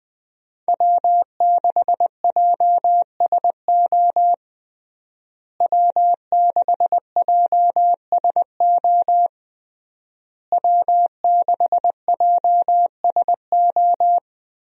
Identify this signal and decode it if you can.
Morse code, heard as a keyed tone, about 21 wpm, 700 Hz. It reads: W6JSO W6JSO W6JSO
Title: USA Call Signs - 5 Characters - Repeated 3x - 20wpm